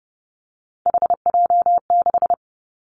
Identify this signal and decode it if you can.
Morse code, heard as a keyed tone, about 32 wpm, 700 Hz. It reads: HJ6
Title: Callsign Prefixes for Top 100 DXCC - 30wpm